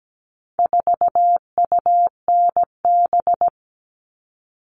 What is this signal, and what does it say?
Morse code, heard as a keyed tone, about 17 wpm, 700 Hz. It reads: 4UNB
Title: All Cut Numbers - 4-Digit 17wpm